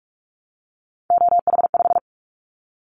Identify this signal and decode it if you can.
Morse code, heard as a keyed tone, about 46 wpm, 700 Hz. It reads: OH5
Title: Callsign Prefixes for Top 50 DXCC - 45wpm